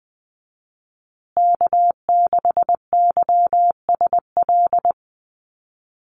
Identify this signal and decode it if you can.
Morse code, heard as a keyed tone, about 20 wpm, 700 Hz. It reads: K6YSL
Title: USA Call Signs Repeated 2x 20wpm